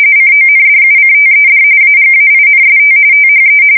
RTTY, about 45.45 baud. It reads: CONDUCTED AGAINST THE
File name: fsk170 / USS-Massachusetts